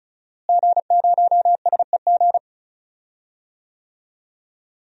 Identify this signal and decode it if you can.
Morse code, heard as a keyed tone, about 36 wpm, 700 Hz. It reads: G0SEG
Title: DXCC-100 Call Signs - 5 Characters - Repeated 3x - 35wpm